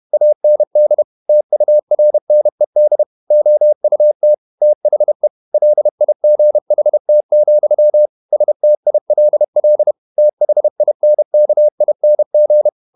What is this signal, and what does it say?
Morse code, heard as a keyed tone, about 31 wpm, 600 Hz. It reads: AND TURNED OUT THE LIGHT, STILL THINKING
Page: Never 31wpm | CW med Gnister